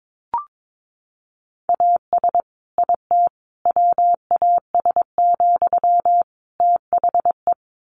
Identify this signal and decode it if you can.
Morse code, heard as a keyed tone, about 22 wpm, 700 Hz. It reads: AS IT WAS, THE
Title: Sentences from Top 1500 Words 22wpm